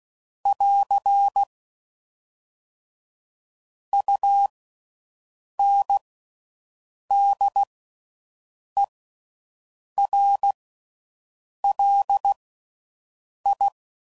Morse code, heard as a keyed tone, about 16 wpm, spaced as for 6 wpm, 800 Hz. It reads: <AR> UNDERLI
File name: Slow With Faster Ings